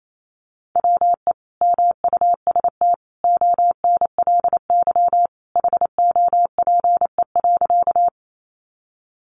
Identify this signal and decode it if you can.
Morse code, heard as a keyed tone, about 28 wpm, 700 Hz. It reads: WE MUST ONLY HOPE.
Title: Sentences from Top 1000 Words 28wpm